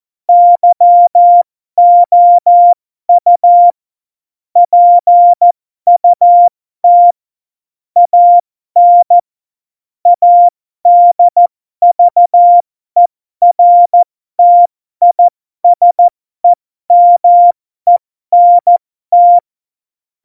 Morse code, heard as a keyed tone, about 14 wpm, spaced as for 12 wpm, 700 Hz. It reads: YOU PUT AN ADVERTISEMENT
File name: Morsekurs / Witze